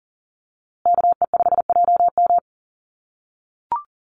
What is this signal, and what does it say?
Morse code, heard as a keyed tone, about 40 wpm, 700 Hz. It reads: KE5JM
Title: USA Call Signs Repeated 2x 40wpm